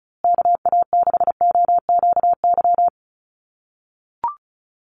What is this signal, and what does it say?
Morse code, heard as a keyed tone, about 34 wpm, 700 Hz. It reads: KA6OQY
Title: USA Call Signs Repeated 2x 35wpm